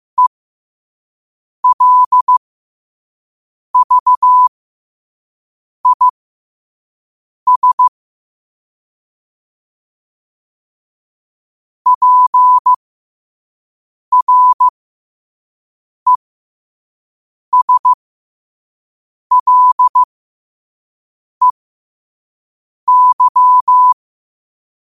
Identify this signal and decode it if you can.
Morse code, heard as a keyed tone, about 15 wpm, spaced as for 5 wpm, 1000 Hz. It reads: ELVIS PRESLEY
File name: qotdfiles / 5wpm